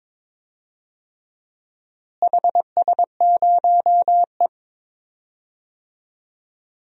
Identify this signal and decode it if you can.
Morse code, heard as a keyed tone, about 22 wpm, 700 Hz. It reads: HS0E
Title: DXCC-100 Call Signs - 4 Characters - Repeated 3x - 22wpm